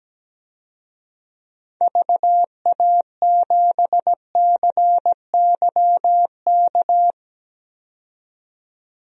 Morse code, heard as a keyed tone, about 17 wpm, 700 Hz. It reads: VA7CYK
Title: DXCC-100 Call Signs - 6 Characters - 17wpm